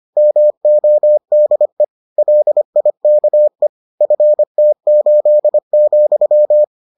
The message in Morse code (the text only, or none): MODE LIKE FT8,